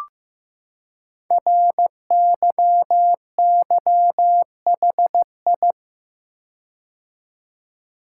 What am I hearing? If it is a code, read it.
Morse code, heard as a keyed tone, about 15 wpm, 700 Hz. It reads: RYYHI
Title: Code Groups - 5 Letters - 15wpm